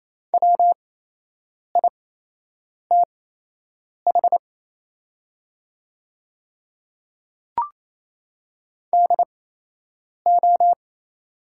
Morse code, heard as a keyed tone, about 28 wpm, spaced as for 8 wpm, 700 Hz. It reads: WITH DO
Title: 8x Character Spacing - Top 300 Words - 28wpm